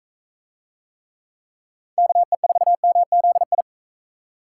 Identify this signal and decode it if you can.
Morse code, heard as a keyed tone, about 45 wpm, 700 Hz. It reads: KE4MZI